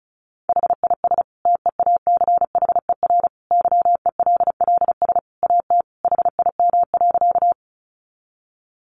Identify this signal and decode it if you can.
Morse code, heard as a keyed tone, about 34 wpm, 700 Hz. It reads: HIS TEACHER YELLS AT HIM.
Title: Additional Easy Phrases 35wpm